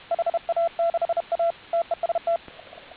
Morse code, heard as a keyed tone, about 31 wpm, 680 Hz, with radio noise. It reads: HA6A TEST